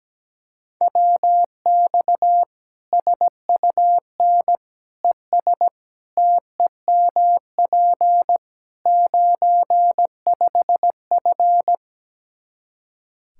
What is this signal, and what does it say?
Morse code, heard as a keyed tone, about 17 wpm, 700 Hz. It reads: WX SUN ES TEMP 95F
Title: K1USN SST QSO Practice 17wpm